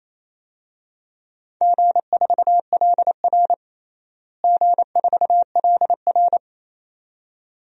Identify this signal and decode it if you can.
Morse code, heard as a keyed tone, about 27 wpm, 700 Hz. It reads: G4LR G4LR
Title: DXCC-100 Call Signs - 4 Characters - Repeated 2x - 28wpm